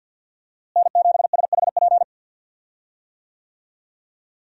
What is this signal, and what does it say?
Morse code, heard as a keyed tone, about 52 wpm, 700 Hz. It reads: N7SHP
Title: USA Call Signs - 5 Characters - Repeated 2x - 50wpm